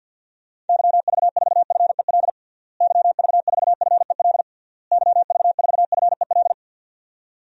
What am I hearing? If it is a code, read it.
Morse code, heard as a keyed tone, about 51 wpm, 700 Hz. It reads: YV4FEL YV4FEL YV4FEL
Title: DXCC-100 Call Signs - 6 Characters - Repeated 3x - 50wpm